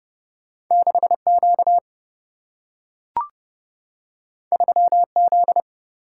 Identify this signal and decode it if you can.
Morse code, heard as a keyed tone, about 30 wpm, 700 Hz. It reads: 6Q 3Z
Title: Single-Letter Single-Number 30wpm